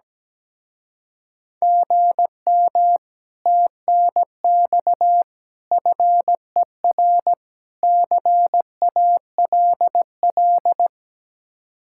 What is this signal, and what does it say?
Morse code, heard as a keyed tone, about 17 wpm, 700 Hz. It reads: GM TNX FER CALL
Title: Speed-Racing Common QSO Phrases Encore - 17wpm